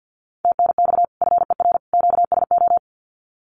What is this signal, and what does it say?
Morse code, heard as a keyed tone, about 50 wpm, 700 Hz. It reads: TNX FER QSO
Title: Common QSO Phrases 50wpm